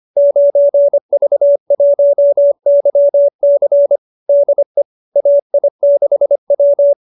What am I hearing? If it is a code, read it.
Morse code, heard as a keyed tone, about 25 wpm, 590 Hz. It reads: 9V1YC DE AI6W
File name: qso207b 25